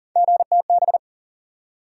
Morse code, heard as a keyed tone, about 38 wpm, 700 Hz. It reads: GTB
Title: USA Call Sign Postfixes - 3 Letters 40wpm